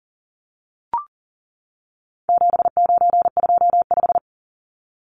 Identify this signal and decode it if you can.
Morse code, heard as a keyed tone, about 37 wpm, 700 Hz. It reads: 7925
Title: Numbers - 4-Digit 40wpm